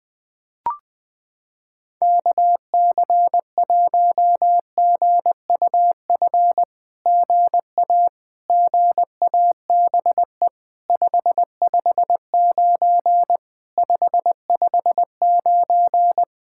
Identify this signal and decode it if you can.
Morse code, heard as a keyed tone, about 20 wpm, 700 Hz. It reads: KC1GUF GA GABE 559 559
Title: SOTA QSO Practice 20wpm